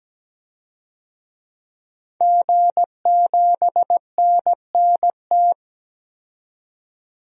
Morse code, heard as a keyed tone, about 17 wpm, 700 Hz. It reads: G7NNT
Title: DXCC-100 Call Signs - 5 Characters - 17wpm